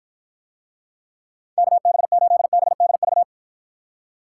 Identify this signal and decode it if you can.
Morse code, heard as a keyed tone, about 53 wpm, 700 Hz. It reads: KB8BDV